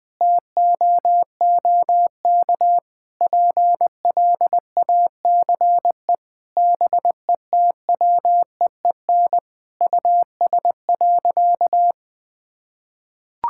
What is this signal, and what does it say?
Morse code, heard as a keyed tone, about 20 wpm, 700 Hz. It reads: TOOK PLACE BETWEEN US.
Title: Sentences from Top 1000 Words - Review Part 6 - 20wpm